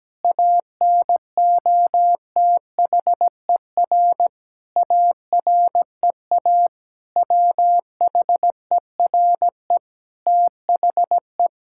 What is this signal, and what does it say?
Morse code, heard as a keyed tone, about 17 wpm, 700 Hz. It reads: ANOTHER AREA WHERE THE
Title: Sentences from Top 1000 Words - Review Part 5 - 17wpm